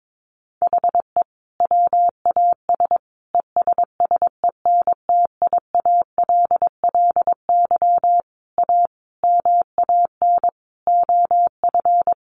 Morse code, heard as a keyed tone, about 22 wpm, 700 Hz. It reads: HE WAS ESSENTIALLY A MAN OF
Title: Sentences from Top 1300 Words 22wpm